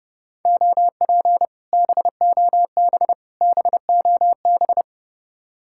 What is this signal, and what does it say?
Morse code, heard as a keyed tone, about 32 wpm, 700 Hz. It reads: OP BOB BOB